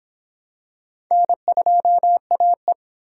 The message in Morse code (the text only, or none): N2AE